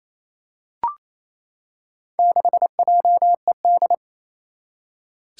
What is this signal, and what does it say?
Morse code, heard as a keyed tone, about 28 wpm, 700 Hz. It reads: 6JED
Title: Four Challenging Letter-Number Combinations 28wpm